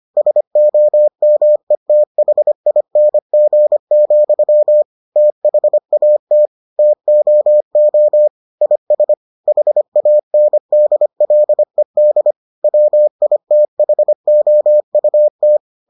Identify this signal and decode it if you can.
Morse code, heard as a keyed tone, about 25 wpm, 600 Hz. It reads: SOMETHING, THAT TOO IS HANDLED WITHOUT